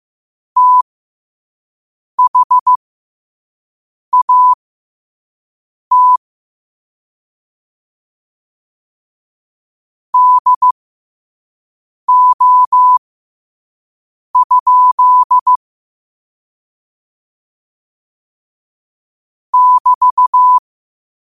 Morse code, heard as a keyed tone, about 15 wpm, spaced as for 5 wpm, 1000 Hz. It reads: THAT DO? <BT>
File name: qotdfiles / 5wpm